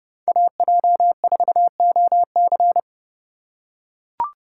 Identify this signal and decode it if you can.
Morse code, heard as a keyed tone, about 30 wpm, 700 Hz. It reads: AJ4OC
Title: USA Call Signs - 5 Characters - 2x Character Spacing - 30wpm